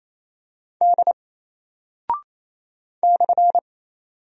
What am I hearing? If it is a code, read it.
Morse code, heard as a keyed tone, about 27 wpm, 700 Hz. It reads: D /
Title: Single Character (Letter:Number:Punctuation:Prosign) 28wpm